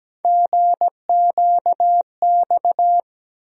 Morse code, heard as a keyed tone, about 17 wpm, 700 Hz. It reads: GQX
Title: USA Call Sign Postfixes - 3 Letters 17wpm